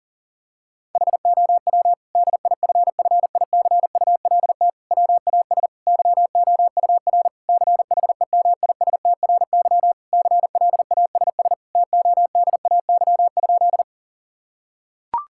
Morse code, heard as a keyed tone, about 40 wpm, 700 Hz. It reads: HOW DIFFICULT WAS YOUR CHEMISTRY CLASS TODAY?
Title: Sentences from Top 1000 Words - Review Part 6 - 40wpm